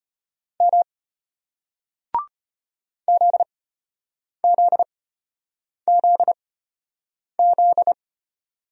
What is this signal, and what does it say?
Morse code, heard as a keyed tone, about 33 wpm, spaced as for 8 wpm, 700 Hz. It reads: M ZZZZ